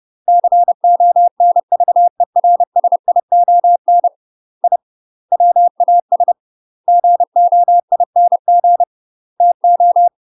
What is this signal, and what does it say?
Morse code, heard as a keyed tone, about 30 wpm, 700 Hz. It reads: CONVERSION I WAS GOING TO
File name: UK MASTER QSOs TRAINING / 30WPM / UK MASTER QSOs 0033